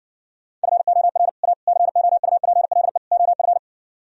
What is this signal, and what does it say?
Morse code, heard as a keyed tone, about 60 wpm, 700 Hz. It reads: FOR A COUPLE OF